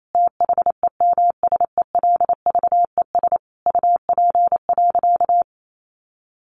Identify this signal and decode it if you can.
Morse code, heard as a keyed tone, about 28 wpm, 700 Hz. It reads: THEMSELVES UP.
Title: Sentences from Top 1400 Words 28wpm